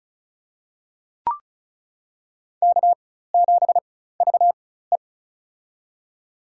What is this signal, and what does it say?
Morse code, heard as a keyed tone, about 36 wpm, spaced as for 16 wpm, 700 Hz. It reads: K7VE
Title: USA Call Signs - 4 Characters - 4x Character Spacing - 35wpm